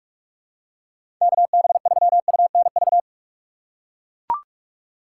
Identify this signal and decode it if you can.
Morse code, heard as a keyed tone, about 42 wpm, 700 Hz. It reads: KB3UNV
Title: USA Call Signs - 6 Characters - 2x Character Spacing - 45wpm